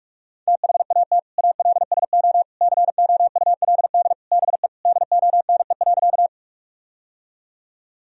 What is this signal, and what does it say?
Morse code, heard as a keyed tone, about 41 wpm, 700 Hz. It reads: THAT ALSO COULD BE DONE.
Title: Sentences from Top 1000 Words - Review Part 1 - 45wpm